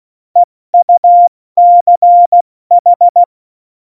Morse code, heard as a keyed tone, about 16 wpm, 700 Hz. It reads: EUCH
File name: Morsekurs / Witze / Witz 02. April 2019, 70 F80